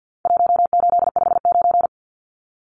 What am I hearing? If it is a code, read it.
Morse code, heard as a keyed tone, about 50 wpm, 700 Hz. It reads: 1859
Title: Numbers - 4-Digit 50wpm